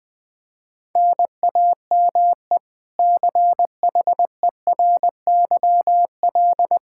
Morse code, heard as a keyed tone, about 21 wpm, 700 Hz. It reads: NAME CHERYL
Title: K1USN SST QSO Practice 20wpm